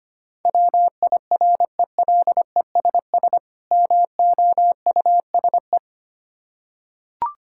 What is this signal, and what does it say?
Morse code, heard as a keyed tone, about 25 wpm, 700 Hz. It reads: WIRELESS MOUSE
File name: practice / 25wpm / Difficult Words with Hints - 25wpm